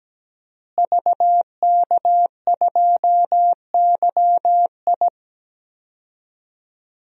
Morse code, heard as a keyed tone, about 17 wpm, 700 Hz. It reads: VK2YI
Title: DXCC-100 Call Signs - 5 Characters - Repeated 2x - 17wpm